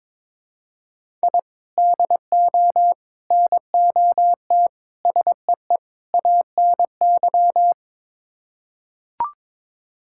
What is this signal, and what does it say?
Morse code, heard as a keyed tone, about 22 wpm, 700 Hz. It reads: I DO NOT SEE ANY
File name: practice / 22wpm / Build A Phrase - 5-Word Phrases - 22wpm